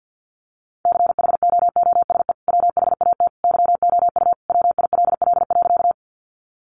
Morse code, heard as a keyed tone, about 48 wpm, 700 Hz. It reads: CHOOSE WHAT YOU WILL.
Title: Sentences from Top 700 Words 50wpm